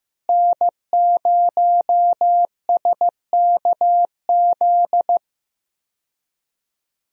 Morse code, heard as a keyed tone, about 15 wpm, 700 Hz. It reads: N0SKZ
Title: USA Call Signs - 5 Characters - Repeated 3x - 15wpm